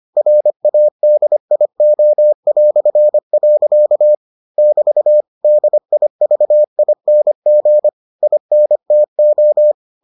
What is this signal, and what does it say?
Morse code, heard as a keyed tone, about 25 wpm, 600 Hz. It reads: RADIO". <BT> DIVING INTO